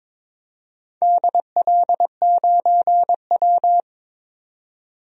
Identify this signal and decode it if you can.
Morse code, heard as a keyed tone, about 22 wpm, 700 Hz. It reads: DL9W
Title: DXCC-100 Call Signs - 4 Characters - 22wpm